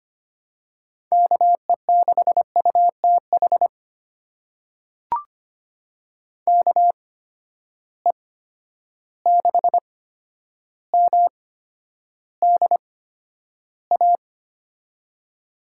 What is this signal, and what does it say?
Morse code, heard as a keyed tone, about 25 wpm, spaced as for 11 wpm, 700 Hz. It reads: KE6UTH KE6MDA
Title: USA Call Signs - 6 Characters - 8x Character Spacing - 25wpm